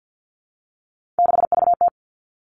Morse code, heard as a keyed tone, about 50 wpm, 700 Hz. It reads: 6VT
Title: Three Challenging Letter-Number Combinations 50wpm